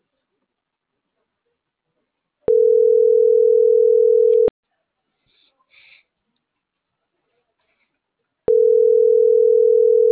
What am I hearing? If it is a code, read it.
Ringback tone (North America), 440 + 480 Hz.
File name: CENTRAL MAERSK / out polanco.msk